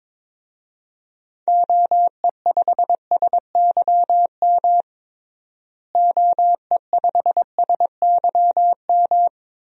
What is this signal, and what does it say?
Morse code, heard as a keyed tone, about 22 wpm, 700 Hz. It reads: OE5SYM OE5SYM
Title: DXCC-100 Call Signs - 6 Characters - Repeated 2x - 22wpm